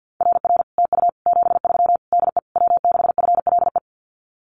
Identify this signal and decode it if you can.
Morse code, heard as a keyed tone, about 50 wpm, 700 Hz. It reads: RR TU 73 DE W6FLE
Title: Common QSO Phrases Encore 50wpm